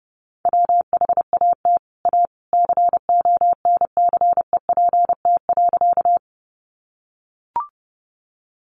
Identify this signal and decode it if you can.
Morse code, heard as a keyed tone, about 30 wpm, 700 Hz. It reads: WHAT A CONCEPT.
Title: Sentences from Top 1100 Words 30wpm